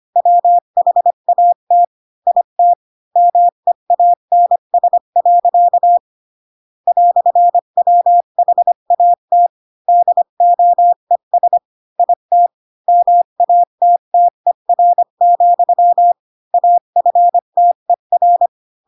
Morse code, heard as a keyed tone, about 25 wpm, 700 Hz. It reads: WHAT IT MEANS. "WHAT DOES IT MATTER, AFTER
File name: Pinocchio / Pinocchio ch8 20 25 ik1hge